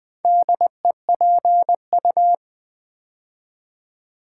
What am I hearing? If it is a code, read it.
Morse code, heard as a keyed tone, about 20 wpm, 700 Hz. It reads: DEPU
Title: Build A Word - 2 Letters At A Time - 6-Letter Words - 20wpm